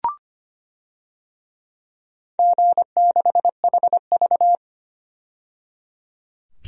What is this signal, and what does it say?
Morse code, heard as a keyed tone, about 25 wpm, 700 Hz. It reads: G6HV
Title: DXCC-100 Call Signs - 4 Characters - 25wpm